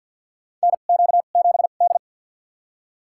Morse code, heard as a keyed tone, about 52 wpm, spaced as for 36 wpm, 700 Hz. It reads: NQ7D
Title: USA Call Signs - 4 Characters - 2x Character Spacing - 50wpm